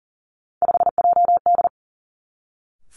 Morse code, heard as a keyed tone, about 42 wpm, 700 Hz. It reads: 5JD
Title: Three Challenging Letter-Number Combinations 40wpm